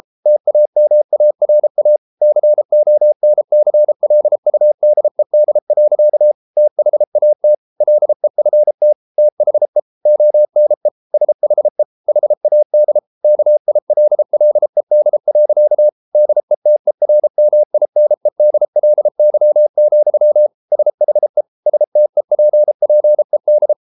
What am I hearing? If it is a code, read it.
Morse code, heard as a keyed tone, about 33 wpm, 600 Hz. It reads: TAMARA CONCLUDED. THAT LEFT THE ONE SHE HAD KILLED. DETERMINEDLY, SHE STEPPED